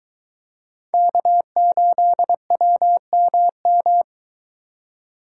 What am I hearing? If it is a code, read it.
Morse code, heard as a keyed tone, about 23 wpm, 700 Hz. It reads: K8WMM